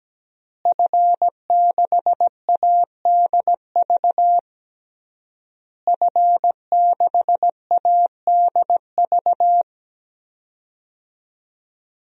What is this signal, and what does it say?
Morse code, heard as a keyed tone, about 17 wpm, 700 Hz. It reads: F6ADV F6ADV
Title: DXCC-100 Call Signs - 5 Characters - Repeated 3x - 17wpm